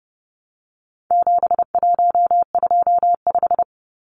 Morse code, heard as a keyed tone, about 30 wpm, 700 Hz. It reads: 7125
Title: Numbers - 4-Digit 30wpm